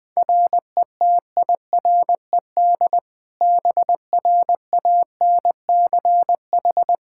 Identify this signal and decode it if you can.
Morse code, heard as a keyed tone, about 21 wpm, 700 Hz. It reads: RETIRED BRANCH
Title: K1USN SST QSO Practice 20wpm